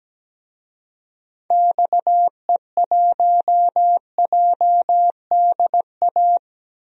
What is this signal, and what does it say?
Morse code, heard as a keyed tone, about 17 wpm, 700 Hz. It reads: XE1JDA
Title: DXCC-100 Call Signs - 6 Characters - 17wpm